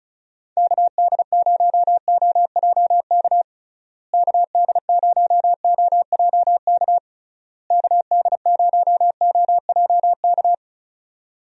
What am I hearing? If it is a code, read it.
Morse code, heard as a keyed tone, about 34 wpm, 700 Hz. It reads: KD0OJK KD0OJK KD0OJK
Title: USA Call Signs - 6 Characters - Repeated 3x - 35wpm